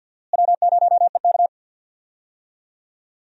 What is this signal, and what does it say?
Morse code, heard as a keyed tone, about 50 wpm, 700 Hz. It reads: W0EK
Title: USA Call Signs - 4 Characters - Repeated 3x - 50wpm